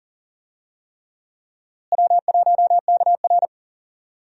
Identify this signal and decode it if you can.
Morse code, heard as a keyed tone, about 40 wpm, 700 Hz. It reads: W1KR
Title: USA Call Signs - 4 Characters - Repeated 3x - 40wpm